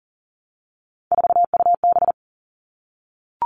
Morse code, heard as a keyed tone, about 40 wpm, 700 Hz. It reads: 4UB
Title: All Cut Numbers - 3-Digit 40wpm